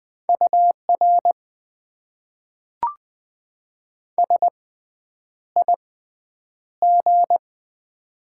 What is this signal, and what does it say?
Morse code, heard as a keyed tone, about 20 wpm, spaced as for 9 wpm, 700 Hz. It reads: UR SIG
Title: Top 90 QSO Elements - 6x Character Spacing - 20wpm